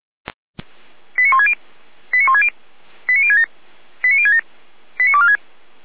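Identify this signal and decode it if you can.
Five-tone selective call (ZVEI), heard as quick runs of five tones, tones of about 70 ms. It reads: 89160 89160 89067 89067 89246
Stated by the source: ZVEI five-tone selective call received by SDR, recorded by Feuerwehr Orlamünde (f85360).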